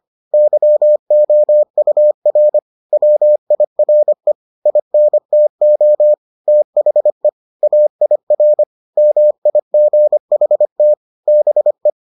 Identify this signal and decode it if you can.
Morse code, heard as a keyed tone, about 25 wpm, 600 Hz. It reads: YOUR WIRE INTO THE AIR MIGHT BE